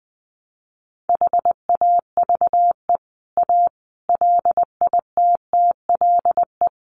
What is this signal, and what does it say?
Morse code, heard as a keyed tone, about 20 wpm, 700 Hz. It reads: HAVE A LITTLE
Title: Sentences from Top 1300 Words 20wpm